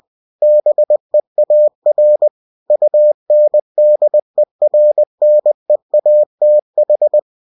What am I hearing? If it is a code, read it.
Morse code, heard as a keyed tone, about 20 wpm, 600 Hz. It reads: BEAR UNDERNEATH
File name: cw-books / winnie-the-pooh / winnie-20wpm-page-0095